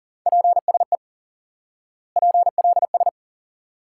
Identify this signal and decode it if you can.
Morse code, heard as a keyed tone, about 40 wpm, 700 Hz. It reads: PSE PLS
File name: IQE Course 3 - Lesson 89 - YAGI - Two Elements - 40wpm